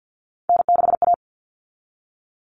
Morse code, heard as a keyed tone, about 50 wpm, 700 Hz. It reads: N6A
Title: All Cut Numbers - 3-Digit 50wpm